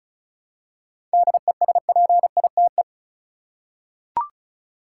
Morse code, heard as a keyed tone, about 34 wpm, 700 Hz. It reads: DESPITE
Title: Seven Letter Words - 35wpm